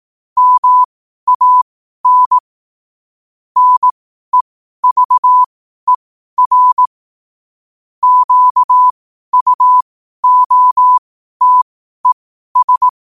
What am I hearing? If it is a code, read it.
Morse code, heard as a keyed tone, about 18 wpm, spaced as for 12 wpm, 1000 Hz. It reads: MAN NEVER QUOTES
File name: qotdfiles / 12wpm